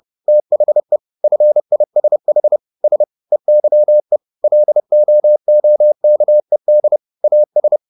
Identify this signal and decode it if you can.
Morse code, heard as a keyed tone, about 30 wpm, 600 Hz. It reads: THE FISH S EYE LOOKED AS